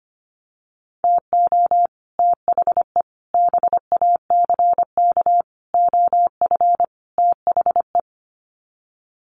Morse code, heard as a keyed tone, about 25 wpm, 700 Hz. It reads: TO THE BACK OF THE
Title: Sets of 5 Words 25wpm